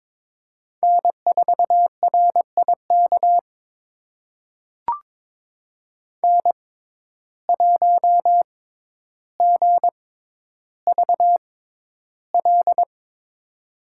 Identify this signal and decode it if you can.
Morse code, heard as a keyed tone, about 22 wpm, spaced as for 11 wpm, 700 Hz. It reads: N4RIK N1GVL
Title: USA Call Signs - 5 Characters - 6x Character Spacing - 22wpm